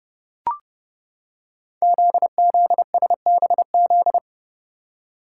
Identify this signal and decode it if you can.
Morse code, heard as a keyed tone, about 30 wpm, 700 Hz. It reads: ZZSBZ
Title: Code Groups - 5 Letters - 30wpm